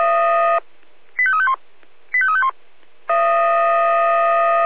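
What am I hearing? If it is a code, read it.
Five-tone selective call (ZVEI), heard as quick runs of five tones, tones of about 70 ms. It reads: 85351 85351
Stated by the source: ZVEI five-tone selective call 85246 received on SDR.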